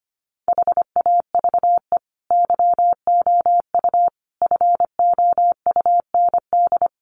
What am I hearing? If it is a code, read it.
Morse code, heard as a keyed tone, about 25 wpm, 700 Hz. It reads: HAVE YOU FOUND
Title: Sentences from Top 300 Words 25wpm